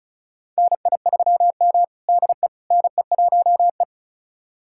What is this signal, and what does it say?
Morse code, heard as a keyed tone, about 36 wpm, 700 Hz. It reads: NI3M DE NE1E
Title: USA Callsign DE Callsign - 4 Characters - 35wpm